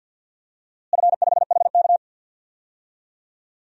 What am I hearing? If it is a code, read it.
Morse code, heard as a keyed tone, about 51 wpm, 700 Hz. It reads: F5HK
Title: DXCC-100 Call Signs - 4 Characters - Repeated 2x - 50wpm